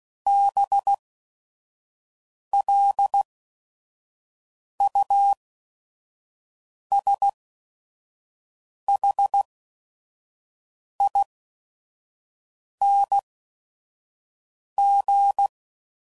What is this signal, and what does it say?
Morse code, heard as a keyed tone, about 16 wpm, spaced as for 5 wpm, 800 Hz. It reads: BLUSHING
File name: Suffixes / Special ING's / VerySlowToFaster ING's